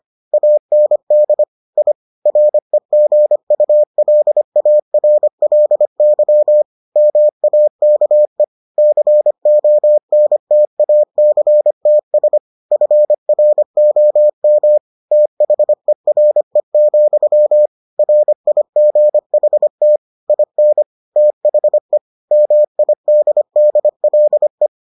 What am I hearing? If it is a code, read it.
Morse code, heard as a keyed tone, about 25 wpm, 600 Hz. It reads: AND I REGULARLY MAKE CONTACTS FROM THERE, RIGHT IN THE MIDDLE